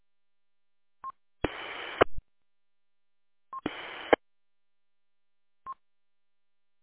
There is no code.